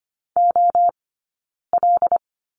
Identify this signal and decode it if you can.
Morse code, heard as a keyed tone, about 25 wpm, 700 Hz. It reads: O L
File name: Lesson 17d - Letter u - Two Characters 25wpm